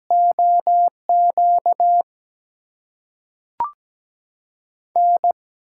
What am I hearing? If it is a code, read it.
Morse code, heard as a keyed tone, about 17 wpm, 700 Hz. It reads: OQ N
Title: USA Call Signs - 5 Characters - 8x Character Spacing - 17wpm